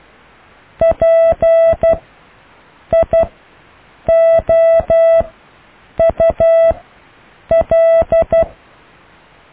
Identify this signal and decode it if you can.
Morse code, heard as a keyed tone, about 12 wpm, spaced as for 7 wpm, 660 Hz. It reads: PIOUL
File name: cw / IJLUM+CDRO+EKTPV+BFHNW